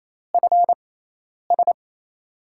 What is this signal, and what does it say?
Morse code, heard as a keyed tone, about 28 wpm, spaced as for 10 wpm, 700 Hz. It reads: FS
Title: USA Call Signs - 5 Characters - 6x Character Spacing - 28wpm